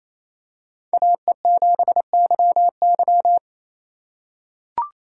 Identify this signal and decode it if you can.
Morse code, heard as a keyed tone, about 28 wpm, 700 Hz. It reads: AE7YY
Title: USA Call Signs - 5 Characters - 2x Character Spacing - 28wpm